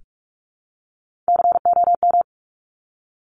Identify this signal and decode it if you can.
Morse code, heard as a keyed tone, about 46 wpm, 700 Hz. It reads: COM
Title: Top Three Letter Combinations in Words 45wpm